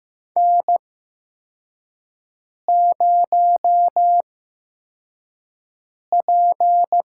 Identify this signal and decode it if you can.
Morse code, heard as a keyed tone, about 15 wpm, spaced as for 4 wpm, 700 Hz. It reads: N0P
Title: USA Call Signs - 4 Characters - 8x Character Spacing - 15wpm